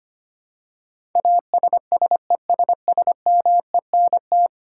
Morse code, heard as a keyed tone, about 25 wpm, 700 Hz. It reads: ASSESSMENT
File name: Ten Letter Words - 25wpm